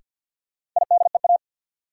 Morse code, heard as a keyed tone, about 53 wpm, 700 Hz. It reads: IDEA
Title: Sentences from Top 300 Words - New Word Drill - 50wpm